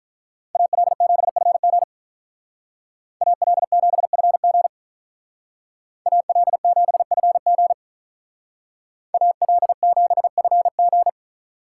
Morse code, heard as a keyed tone, about 44 wpm, 700 Hz. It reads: AL7FG AL7FG AL7FG AL7FG